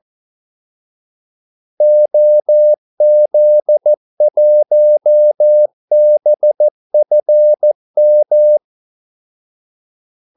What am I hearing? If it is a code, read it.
Morse code, heard as a keyed tone, about 14 wpm, 600 Hz. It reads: OZ1BFM
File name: Callsigns-14wpm